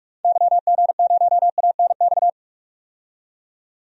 Morse code, heard as a keyed tone, about 47 wpm, 700 Hz. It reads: YG0ANX
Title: DXCC-100 Call Signs - 6 Characters - 45wpm